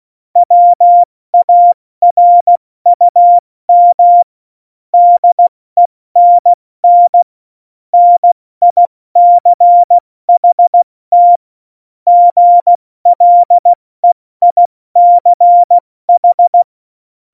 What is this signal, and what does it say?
Morse code, heard as a keyed tone, about 16 wpm, spaced as for 14 wpm, 700 Hz. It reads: WARUM DENN NICHT GLEICH
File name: Morsekurs / Witze / Witz 05. Januar 2021, 70 F80